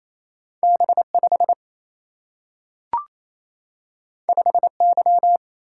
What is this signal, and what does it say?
Morse code, heard as a keyed tone, about 28 wpm, 700 Hz. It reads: B 5 5 Y
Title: Single-Letter Single-Number 28wpm